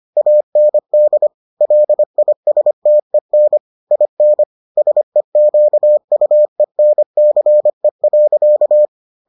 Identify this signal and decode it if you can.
Morse code, heard as a keyed tone, about 25 wpm, 600 Hz. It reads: AND LISTEN IN SEQUENCE.